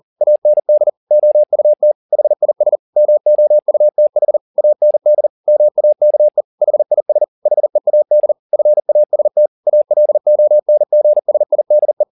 Morse code, heard as a keyed tone, about 40 wpm, 600 Hz. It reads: AND OUT HIS MOUTH AND MAKE HIS HEAD FAST ALONGSIDE
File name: the old man and the sea 2 40wpm